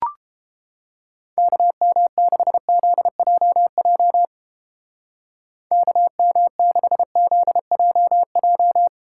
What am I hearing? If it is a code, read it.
Morse code, heard as a keyed tone, about 30 wpm, 700 Hz. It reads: KM6ZJJ KM6ZJJ